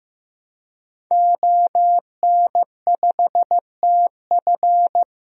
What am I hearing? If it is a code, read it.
Morse code, heard as a keyed tone, about 15 wpm, 700 Hz. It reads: ON5TF
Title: DXCC-100 Call Signs - 5 Characters - Repeated 2x - 15wpm